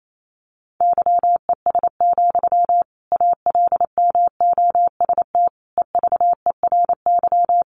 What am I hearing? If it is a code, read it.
Morse code, heard as a keyed tone, about 28 wpm, 700 Hz. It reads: YES, ALMOST EVERY
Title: Sentences from Top 700 Words 28wpm